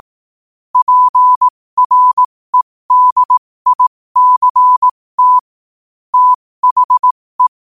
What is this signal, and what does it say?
Morse code, heard as a keyed tone, about 18 wpm, spaced as for 15 wpm, 1000 Hz. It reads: PREDICT THE
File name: qotdfiles / 15wpm